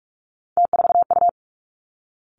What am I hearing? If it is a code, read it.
Morse code, heard as a keyed tone, about 46 wpm, 700 Hz. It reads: T4U
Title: Three Challenging Letter-Number Combinations 45wpm